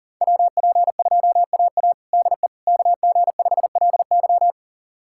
Morse code, heard as a keyed tone, about 40 wpm, 700 Hz. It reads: WP2AA DE KG5LY
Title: USA Callsign DE Callsign - 5 Characters - 40wpm